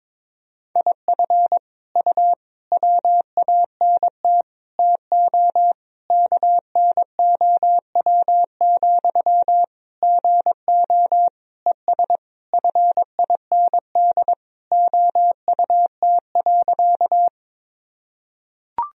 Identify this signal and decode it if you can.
Morse code, heard as a keyed tone, about 21 wpm, 700 Hz. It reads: IF U WANT TO KNOW, GO ES FIND OUT.
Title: Sentences from Top 200 Words and Abbreviations 22wpm